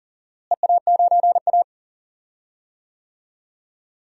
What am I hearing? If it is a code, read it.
Morse code, heard as a keyed tone, about 37 wpm, 700 Hz. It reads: EA9A
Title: DXCC-100 Call Signs - 4 Characters - Repeated 2x - 40wpm